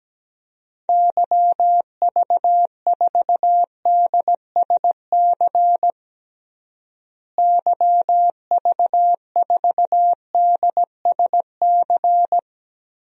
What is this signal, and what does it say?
Morse code, heard as a keyed tone, about 17 wpm, 700 Hz. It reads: YV4DSC YV4DSC
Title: DXCC-100 Call Signs - 6 Characters - Repeated 2x - 17wpm